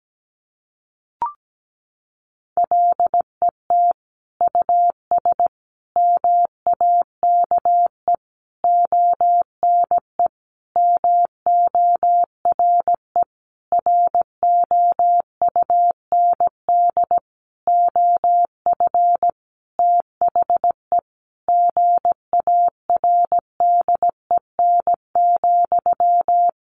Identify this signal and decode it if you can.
Morse code, heard as a keyed tone, about 17 wpm, 700 Hz. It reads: LET US MAKE ONE MORE ROUND OF THE GARDEN,
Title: Sentences from Top 700 Words 17wpm